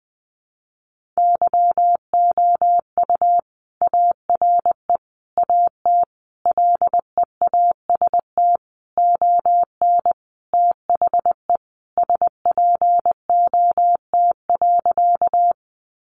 Morse code, heard as a keyed tone, about 20 wpm, 700 Hz. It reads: YOU ARE AT LEAST ON THE SPOT.
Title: Sentences from Top 1100 Words 20wpm